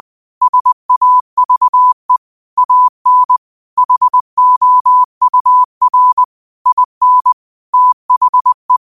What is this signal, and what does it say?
Morse code, heard as a keyed tone, about 20 wpm, 1000 Hz. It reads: SAVE AN HOUR IN THE
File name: qotdfiles / 20wpm